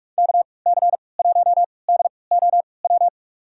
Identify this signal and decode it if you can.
Morse code, heard as a keyed tone, about 42 wpm, spaced as for 25 wpm, 700 Hz. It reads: KC1DOW
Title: USA Call Signs - 6 Characters - 3x Character Spacing - 45wpm